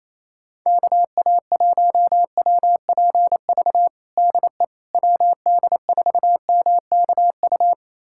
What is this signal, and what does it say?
Morse code, heard as a keyed tone, about 28 wpm, 700 Hz. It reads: KA1WPV DE WD4MKU
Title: USA Callsign DE Callsign - 6 Characters - 28wpm